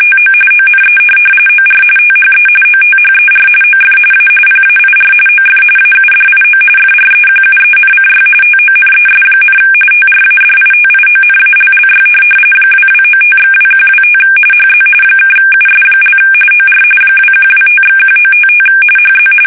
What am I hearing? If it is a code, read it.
RTTY, about 45.45 baud. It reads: TO ONE OR THE OTHER OF A PAIR OF AUDIO FREQUENCY TONES, TRADITIONALLY 2295/2125 HZ (US) OR 2125/1955 HZ
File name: fsk850 / teletype-RTTY